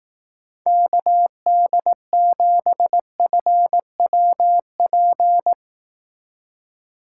Morse code, heard as a keyed tone, about 18 wpm, 700 Hz. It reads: KD7FWP
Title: Speed Racing - USA 2x3 Call Signs 15wpm